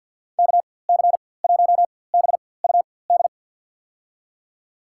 Morse code, heard as a keyed tone, about 49 wpm, spaced as for 23 wpm, 700 Hz. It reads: KC1BUD
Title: USA Call Signs - 6 Characters - 4x Character Spacing - 50wpm